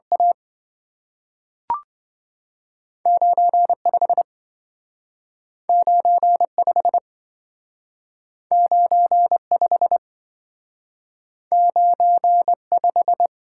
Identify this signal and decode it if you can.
Morse code, heard as a keyed tone, about 27 wpm, 700 Hz. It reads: A 95 95 95 95